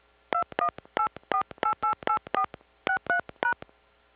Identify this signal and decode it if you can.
DTMF, heard as telephone touch tones, tones of about 90 ms. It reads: 2184888463#